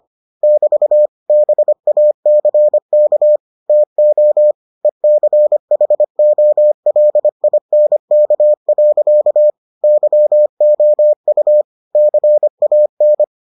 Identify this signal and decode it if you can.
Morse code, heard as a keyed tone, about 25 wpm, 600 Hz. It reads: <BT> BACK TO ECHOLINK. YOU CAN